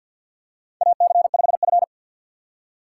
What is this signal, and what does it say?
Morse code, heard as a keyed tone, about 51 wpm, 700 Hz. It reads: AC5F
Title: USA Call Signs - 4 Characters - 6x Character Spacing - 50wpm